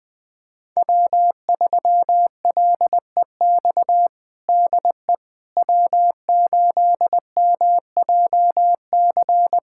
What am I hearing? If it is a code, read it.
Morse code, heard as a keyed tone, about 21 wpm, 700 Hz. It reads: W3LEX DE W8MJC
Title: USA Callsign DE Callsign - 5 Characters - 20wpm